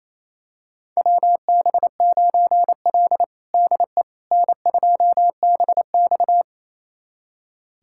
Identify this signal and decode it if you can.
Morse code, heard as a keyed tone, about 28 wpm, 700 Hz. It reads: WB9L DE N2BX
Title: USA Callsign DE Callsign - 4 Characters - 28wpm